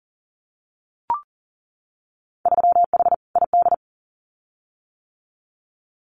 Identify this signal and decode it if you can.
Morse code, heard as a keyed tone, about 40 wpm, 700 Hz. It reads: 3H ID
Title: Winter Field Day Exchanges Encore 40wpm